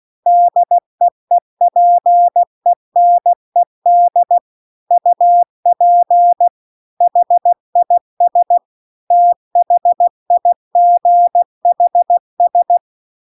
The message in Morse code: DEEPENED UP HIS THIGHS

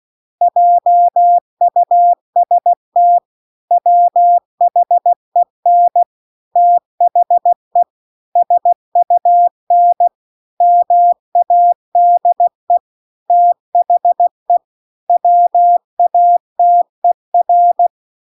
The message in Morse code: JUST WHEN THE SUN MADE THE WATER

